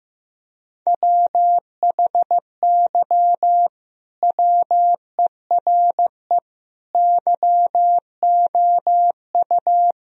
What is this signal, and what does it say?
Morse code, heard as a keyed tone, about 15 wpm, 700 Hz. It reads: WHY WERE YOU
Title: Sentences from Top 1000 Words - Review Part 4 - 15wpm